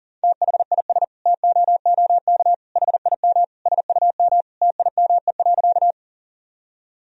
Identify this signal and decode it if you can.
Morse code, heard as a keyed tone, about 40 wpm, 700 Hz. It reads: THIS TOOK HIM SUM TIME.
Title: Sentences from Top 300 Words and Abbreviations 40wpm